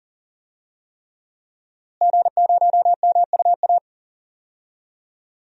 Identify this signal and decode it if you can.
Morse code, heard as a keyed tone, about 40 wpm, 700 Hz. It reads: G0MUA
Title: DXCC-100 Call Signs - 5 Characters - Repeated 3x - 40wpm